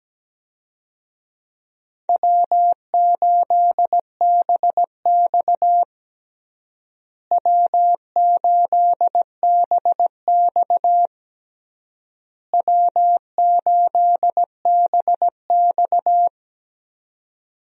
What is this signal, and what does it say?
Morse code, heard as a keyed tone, about 17 wpm, 700 Hz. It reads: W8BX W8BX W8BX
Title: USA Call Signs - 4 Characters - Repeated 3x - 17wpm